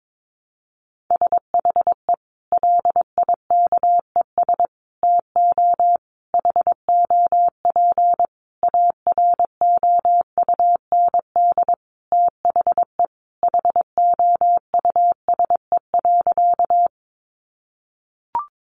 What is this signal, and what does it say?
Morse code, heard as a keyed tone, about 22 wpm, 700 Hz. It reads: SHE LIKES TO HOP AROUND THE HOUSE.
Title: Easy Phrases 22wpm